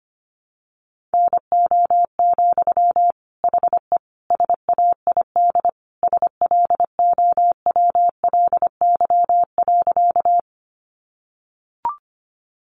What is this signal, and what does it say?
Morse code, heard as a keyed tone, about 25 wpm, 690 Hz. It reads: NO, HE SAID SLOWLY.
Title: Sentences from Top 1100 Words 25wpm